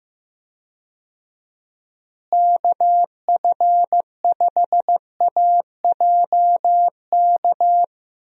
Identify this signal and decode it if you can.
Morse code, heard as a keyed tone, about 15 wpm, 700 Hz. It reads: KF5AJK
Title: USA Call Signs - 6 Characters - Repeated 2x - 15wpm